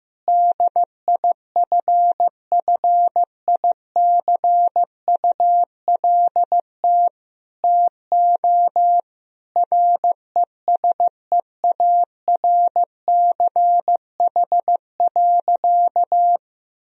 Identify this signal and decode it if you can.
Morse code, heard as a keyed tone, about 15 wpm, 700 Hz. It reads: DIFFICULT TO RESEARCH.
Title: Sentences from Top 1000 Words - Review Part 4 - 15wpm